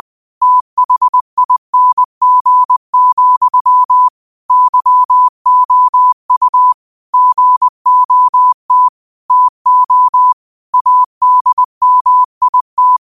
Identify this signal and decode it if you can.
Morse code, heard as a keyed tone, about 20 wpm, 1000 Hz. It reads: THING, YOU GOT TO ADMIT
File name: qotdfiles / 20wpm